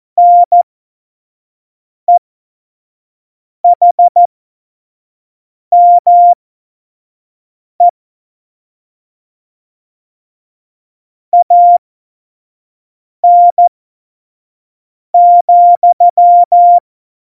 Morse code, heard as a keyed tone, about 14 wpm, spaced as for 5 wpm, 700 Hz. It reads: NEHME AN,
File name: Morsekurs / Witze / Witz 16. April 2019, 25 F70